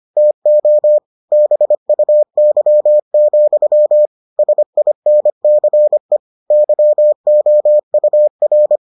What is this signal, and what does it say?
Morse code, heard as a keyed tone, about 25 wpm, 600 Hz. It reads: TO BUY, SINCE YOUR